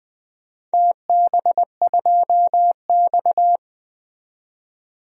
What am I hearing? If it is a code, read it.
Morse code, heard as a keyed tone, about 20 wpm, 700 Hz. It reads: TB2X
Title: DXCC-100 Call Signs - 4 Characters - 20wpm